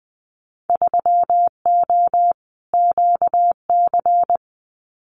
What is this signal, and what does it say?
Morse code, heard as a keyed tone, about 20 wpm, 700 Hz. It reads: 3O QC
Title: Winter Field Day Exchanges Encore 20wpm